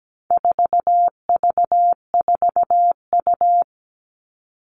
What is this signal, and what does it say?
Morse code, heard as a keyed tone, about 17 wpm, 700 Hz. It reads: 4V4U
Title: All Cut Numbers - 4-Digit 17wpm